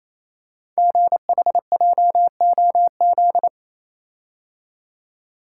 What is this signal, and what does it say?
Morse code, heard as a keyed tone, about 27 wpm, 700 Hz. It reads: GHJOZ
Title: Code Groups - 5 Letters - 28wpm